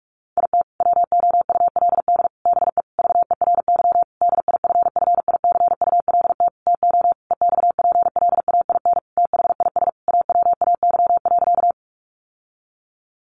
Morse code, heard as a keyed tone, about 44 wpm, 700 Hz. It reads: IT WOULD BE VERY DIFFICULT TO EXPLAIN THIS AWAY.
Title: Sentences from Top 1100 Words 45wpm